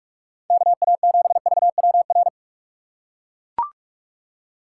Morse code, heard as a keyed tone, about 42 wpm, 700 Hz. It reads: KA7VWR
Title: USA Call Signs - 6x Character Spacing - 45wpm